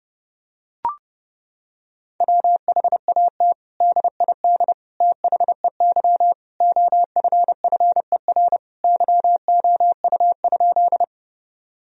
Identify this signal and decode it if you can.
Morse code, heard as a keyed tone, about 32 wpm, 700 Hz. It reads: WHAT DID THEY OFFER YOU?
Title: Sentences from Top 1000 Words - Review Part 4 - 30wpm